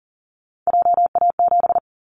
Morse code, heard as a keyed tone, about 40 wpm, 700 Hz. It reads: JA7
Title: Callsign Prefixes for Top 50 DXCC - 40wpm